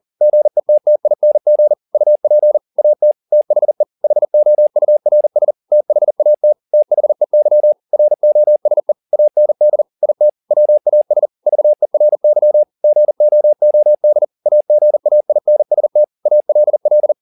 Morse code, heard as a keyed tone, about 40 wpm, 600 Hz. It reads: GETTING UP AT THE HOURS THAT THEY ROSE AND IT WAS VERY GOOD AGAINST ALL